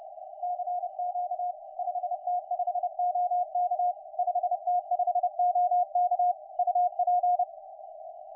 Morse code, heard as a keyed tone, about 32 wpm, 720 Hz, with radio noise. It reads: CQ 5T5OK 5T5OK UP
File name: 5T5OK 10CW